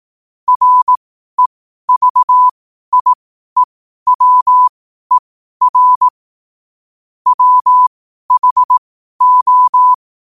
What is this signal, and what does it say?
Morse code, heard as a keyed tone, about 18 wpm, spaced as for 12 wpm, 1000 Hz. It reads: REVIEWER WHO
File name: qotdfiles / 12wpm